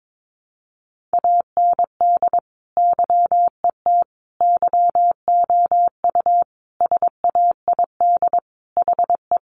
Morse code, heard as a keyed tone, about 22 wpm, 700 Hz. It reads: AND YET YOU SAID HE
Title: Sentences from Top 500 Words 22wpm